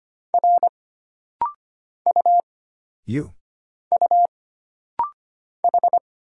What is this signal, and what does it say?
Morse code, heard as a keyed tone, about 25 wpm, 700 Hz. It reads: R U U H
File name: Single Character (Letter:Number:Punctuation:Prosign) - Rapid-Fire - 25wpm